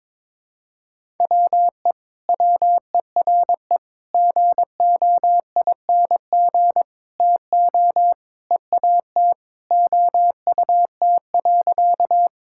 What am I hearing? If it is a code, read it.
Morse code, heard as a keyed tone, about 22 wpm, 700 Hz. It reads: WE WERE GOING TO EAT OUT.
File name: practice / 22wpm / Sentences with 4-Word Phrases - 22wpm